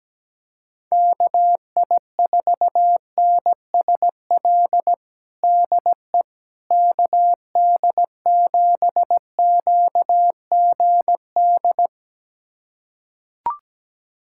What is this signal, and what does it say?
Morse code, heard as a keyed tone, about 17 wpm, 700 Hz. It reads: KI4NSL DE KD7QGD
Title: USA Callsign DE Callsign - 6 Characters - 17wpm